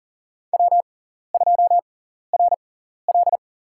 Morse code, heard as a keyed tone, about 40 wpm, spaced as for 14 wpm, 700 Hz. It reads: W2RL
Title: USA Call Signs - 4 Characters - 6x Character Spacing - 40wpm